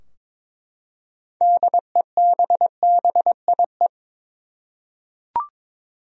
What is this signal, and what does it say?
Morse code, heard as a keyed tone, about 22 wpm, 700 Hz. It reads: DEBBIE
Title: USA Common Names - 22wpm